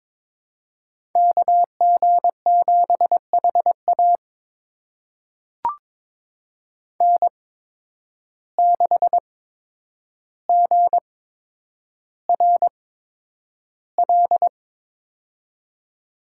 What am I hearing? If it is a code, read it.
Morse code, heard as a keyed tone, about 22 wpm, spaced as for 9 wpm, 700 Hz. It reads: KG7HA N6GRL
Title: USA Call Signs - 5 Characters - 8x Character Spacing - 22wpm